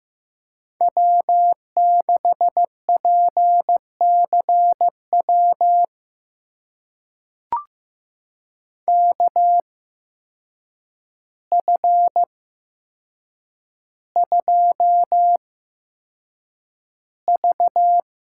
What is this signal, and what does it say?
Morse code, heard as a keyed tone, about 15 wpm, spaced as for 7 wpm, 700 Hz. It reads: W6PCW KF2V
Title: USA Call Signs - 5 Characters - 8x Character Spacing - 15wpm